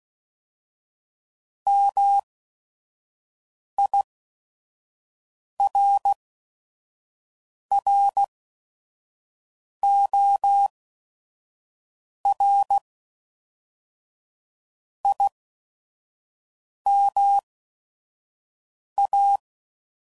Morse code, heard as a keyed tone, about 16 wpm, spaced as for 5 wpm, 800 Hz. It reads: MIRROR IMA
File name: CW Code Groups @5WPM